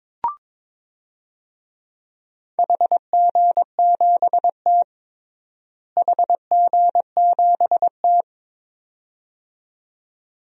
Morse code, heard as a keyed tone, about 22 wpm, 700 Hz. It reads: HG7T HG7T
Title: DXCC-100 Call Signs - 4 Characters - Repeated 2x - 22wpm